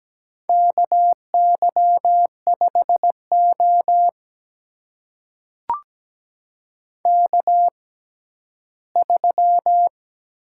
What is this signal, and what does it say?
Morse code, heard as a keyed tone, about 17 wpm, spaced as for 11 wpm, 700 Hz. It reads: KY5O K3
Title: USA Call Signs - 4 Characters - 6x Character Spacing - 17wpm